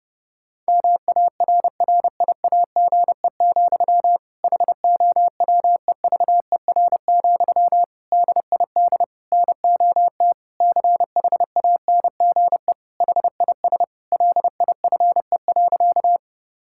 Morse code, heard as a keyed tone, about 30 wpm, 700 Hz. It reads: MARRIAGE, HOWEVER, DID NOT CHANGE HIS LIFE.
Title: Sentences from Top 1000 Words - Review Part 5 - 30wpm